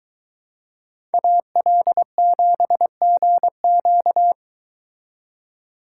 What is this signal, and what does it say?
Morse code, heard as a keyed tone, about 23 wpm, 700 Hz. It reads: AL7GQ